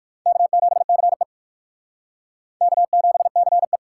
Morse code, heard as a keyed tone, about 68 wpm, spaced as for 58 wpm, 700 Hz. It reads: K7CE K7CE